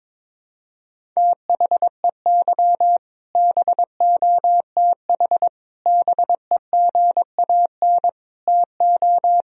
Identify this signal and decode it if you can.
Morse code, heard as a keyed tone, about 22 wpm, 700 Hz. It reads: THEY BOTH BEGAN TO
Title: Sentences from Top 1000 Words - Review Part 1 - 22wpm